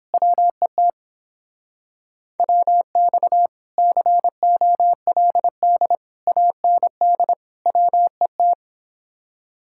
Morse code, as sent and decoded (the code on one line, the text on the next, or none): .-- . - / .-- -..- / -.-. --- .-.. -.. / .- -. -.. / .-- . -
WET WX COLD AND WET